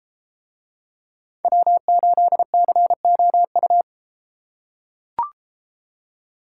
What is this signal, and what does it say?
Morse code, heard as a keyed tone, about 31 wpm, 700 Hz. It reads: W8COU